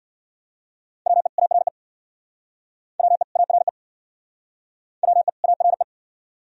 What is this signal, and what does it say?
Morse code, heard as a keyed tone, about 68 wpm, 700 Hz. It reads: WE ARE WE ARE WE ARE